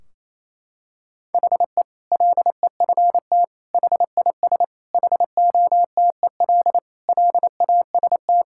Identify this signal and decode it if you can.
Morse code, heard as a keyed tone, about 28 wpm, 700 Hz. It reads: HE LEFT HIS HOTEL LAST
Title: Sentences from Top 1600 Words 28wpm